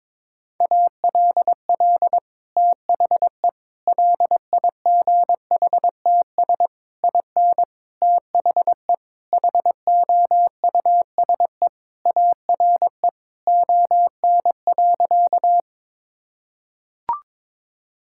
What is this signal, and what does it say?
Morse code, heard as a keyed tone, about 22 wpm, 700 Hz. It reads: ALL THE LIGHTS IN THE HOUSE ARE ON.
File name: practice / 22wpm / Sentences with 2-Word Phrases - 22wpm